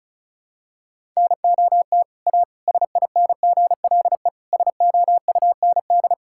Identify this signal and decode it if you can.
Morse code, heard as a keyed tone, about 36 wpm, 700 Hz. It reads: NOT A SINGLE SOUND
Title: Sentences from Top 1000 Words - Review Part 5 - 35wpm